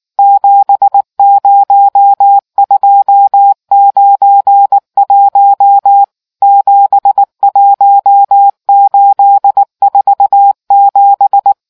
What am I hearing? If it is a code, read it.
Morse code, heard as a keyed tone, about 19 wpm, 800 Hz. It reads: 70291 71847